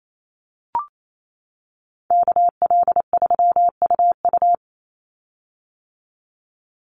Morse code, heard as a keyed tone, about 28 wpm, 700 Hz. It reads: KL3UU
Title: USA Call Signs 28wpm